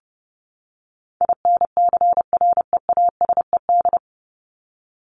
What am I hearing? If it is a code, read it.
Morse code, heard as a keyed tone, about 30 wpm, 700 Hz. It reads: INCREASED
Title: ED Suffix - Top 100 Words with Base Words 30wpm